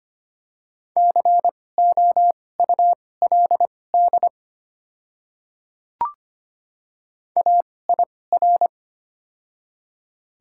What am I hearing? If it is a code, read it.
Morse code, heard as a keyed tone, about 25 wpm, spaced as for 18 wpm, 690 Hz. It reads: COULD AIR
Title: 2x Character Spacing - Top 300 Words - 25wpm